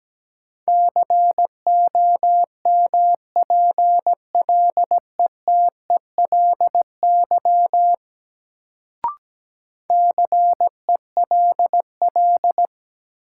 Morse code, heard as a keyed tone, about 17 wpm, 700 Hz. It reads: COMPLETELY CELL
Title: Sentences from Top 1100 Words - New Word Drill Alt - 17wpm